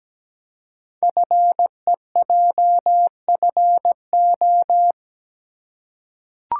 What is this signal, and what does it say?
Morse code, heard as a keyed tone, about 17 wpm, 700 Hz. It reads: FEJFO
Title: Code Groups - 5 Letters - 17wpm